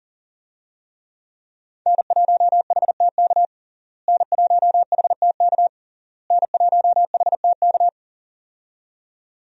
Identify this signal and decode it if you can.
Morse code, heard as a keyed tone, about 40 wpm, 700 Hz. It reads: N1HTK N1HTK N1HTK
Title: USA Call Signs - 5 Characters - Repeated 3x - 40wpm